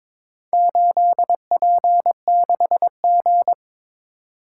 Morse code, heard as a keyed tone, about 22 wpm, 700 Hz. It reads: 8P6G
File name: practice / 22wpm / Four Letter-Number Combinations - 22wpm